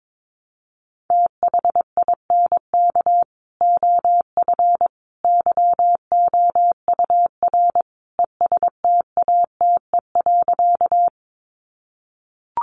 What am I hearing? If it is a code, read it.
Morse code, heard as a keyed tone, about 22 wpm, 700 Hz. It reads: THINK OF YOUR ESTATE.
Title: Sentences from Top 1200 Words 22wpm